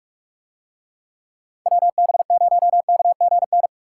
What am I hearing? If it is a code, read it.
Morse code, heard as a keyed tone, about 44 wpm, 700 Hz. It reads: WB0KGN